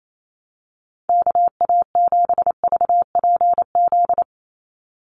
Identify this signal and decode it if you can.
Morse code, heard as a keyed tone, about 28 wpm, 700 Hz. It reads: KA7VPZ
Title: USA Call Signs 28wpm